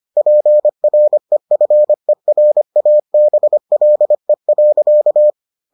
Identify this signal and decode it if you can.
Morse code, heard as a keyed tone, about 25 wpm, 600 Hz. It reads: PREFERABLE.